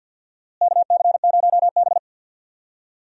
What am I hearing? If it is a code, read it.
Morse code, heard as a keyed tone, about 53 wpm, 700 Hz. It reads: KC0B
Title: USA Call Signs - 4 Characters - Repeated 2x - 50wpm